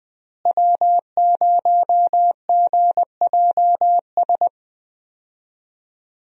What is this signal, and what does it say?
Morse code, heard as a keyed tone, about 20 wpm, 700 Hz. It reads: W0GJS
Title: USA Call Signs - 5 Characters - Repeated 3x - 20wpm